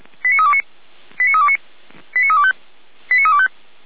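Five-tone selective call (ZVEI), heard as quick runs of five tones, tones of about 80 ms. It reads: 89329 89329 89325 89325
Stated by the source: ZVEI five-tone selective call received on an SDR.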